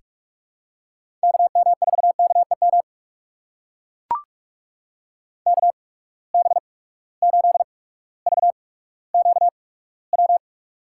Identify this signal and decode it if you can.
Morse code, heard as a keyed tone, about 44 wpm, spaced as for 19 wpm, 700 Hz. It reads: KM4KEM KB8VQW
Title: USA Call Signs - 6 Characters - 8x Character Spacing - 45wpm